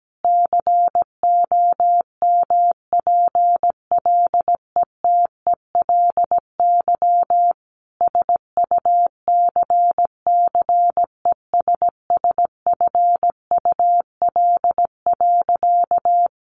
Morse code, heard as a keyed tone, about 17 wpm, 700 Hz. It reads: COMPLETELY SUCCESSFUL.
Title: Sentences from Top 1100 Words 17wpm